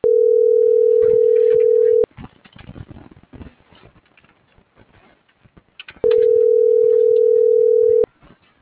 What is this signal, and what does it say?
Ringback tone (North America), 440 + 480 Hz.